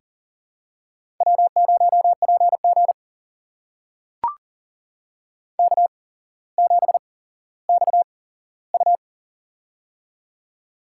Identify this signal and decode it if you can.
Morse code, heard as a keyed tone, about 40 wpm, spaced as for 17 wpm, 700 Hz. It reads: W0PG K7XU
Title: USA Call Signs - 4 Characters - 8x Character Spacing - 40wpm